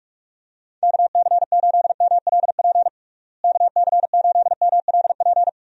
Morse code, heard as a keyed tone, about 46 wpm, 700 Hz. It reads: KC8MLP KC8MLP
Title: USA Call Signs - 6 Characters - Repeated 3x - 45wpm